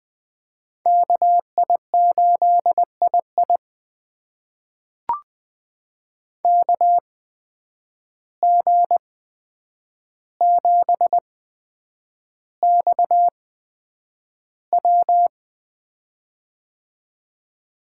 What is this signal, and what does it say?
Morse code, heard as a keyed tone, about 20 wpm, spaced as for 9 wpm, 700 Hz. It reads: KI8II KG7XW
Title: USA Call Signs - 5 Characters - 8x Character Spacing - 20wpm